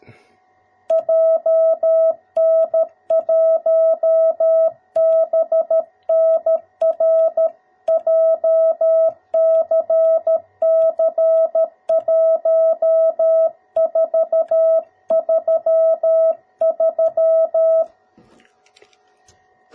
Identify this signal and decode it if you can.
Morse code, heard as a keyed tone, about 13 wpm, 650 Hz. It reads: JN1BNR JCC1433
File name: jh8cbh / 2021cw / cw02b